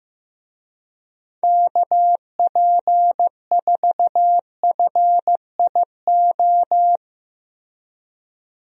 Morse code, heard as a keyed tone, about 15 wpm, 700 Hz. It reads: KP4FIO
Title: USA Call Signs - 6 Characters - Repeated 3x - 15wpm